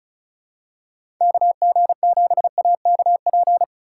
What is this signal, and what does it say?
Morse code, heard as a keyed tone, about 34 wpm, 700 Hz. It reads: KG7AKP
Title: USA Call Signs - 6 Characters - 2x Character Spacing - 35wpm